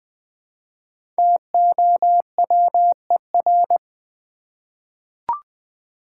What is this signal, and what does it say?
Morse code, heard as a keyed tone, about 20 wpm, 700 Hz. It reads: TOWER
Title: 5 Letter Words - First 3 Letters Given - 20wpm